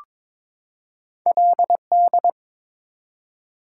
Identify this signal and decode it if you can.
Morse code, heard as a keyed tone, about 22 wpm, 690 Hz. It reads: LD
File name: practice / 22wpm / Two Letter-Number Combinations - 22wpm